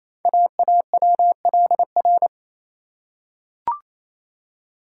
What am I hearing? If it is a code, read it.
Morse code, heard as a keyed tone, about 28 wpm, 700 Hz. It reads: AAWLR
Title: Code Groups - 5 Letters - 28wpm